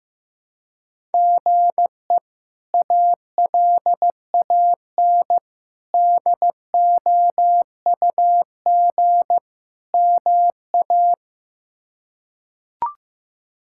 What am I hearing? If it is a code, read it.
Morse code, heard as a keyed tone, about 15 wpm, 700 Hz. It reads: GE ALAN DOUG MA
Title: K1USN SST QSO Practice - 15wpm